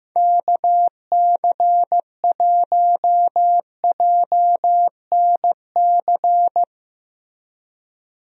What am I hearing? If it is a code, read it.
Morse code, heard as a keyed tone, about 16 wpm, 700 Hz. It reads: KC1JNC
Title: USA Call Signs - 6 Characters - Repeated 2x - 15wpm